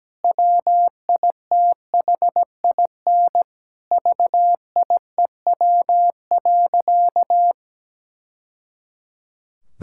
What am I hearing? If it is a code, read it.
Morse code, heard as a keyed tone, about 17 wpm, 700 Hz. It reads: WITHIN VIEW.
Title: Sentences from Top 1000 Words - Review Part 6 - 17wpm